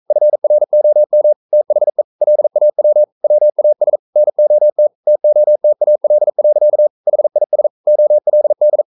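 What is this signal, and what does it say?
Morse code, heard as a keyed tone, about 42 wpm, 600 Hz. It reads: FROM THE LAW WAS NOT TOTAL. HIS OLD
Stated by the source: Morse code practice at 41 wpm, from CW med Gnister.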